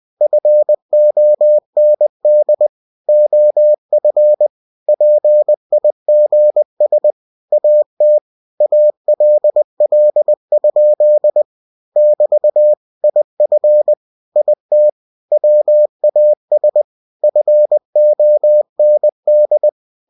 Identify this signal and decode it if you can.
Morse code, heard as a keyed tone, about 20 wpm, 600 Hz. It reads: FOND OF PIGS AT ALL? <BT> IF IT WAS FOND
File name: cw-books / winnie-the-pooh / winnie-20wpm-page-0041